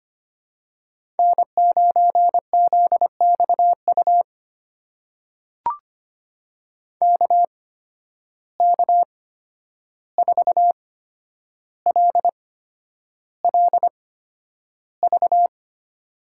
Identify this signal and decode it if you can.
Morse code, heard as a keyed tone, about 25 wpm, spaced as for 10 wpm, 700 Hz. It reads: N9ZXU KK4LLV
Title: USA Call Signs - 8x Character Spacing - 25wpm